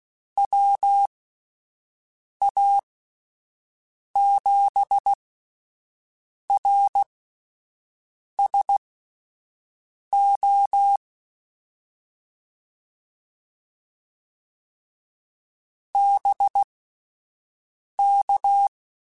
Morse code, heard as a keyed tone, about 16 wpm, spaced as for 5 wpm, 800 Hz. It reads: WA7RSO BK
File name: QSOs / QSOs-mp3s / WD0FBT 1@10